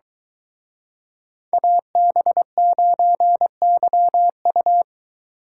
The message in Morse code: AB9YU

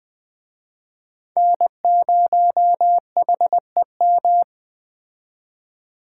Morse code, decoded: N0HEM

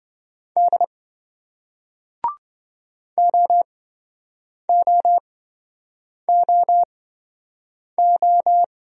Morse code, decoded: D OOOO

